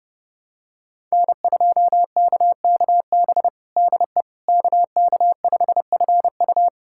Morse code, heard as a keyed tone, about 30 wpm, 700 Hz. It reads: N2KKB DE KK5FU
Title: USA Callsign DE Callsign 30wpm